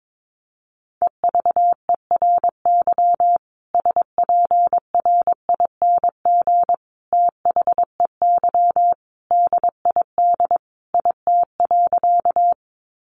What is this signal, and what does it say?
Morse code, heard as a keyed tone, about 22 wpm, 700 Hz. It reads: EVERY SPRING THEY DID IT.
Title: Sentences from Top 1200 Words 22wpm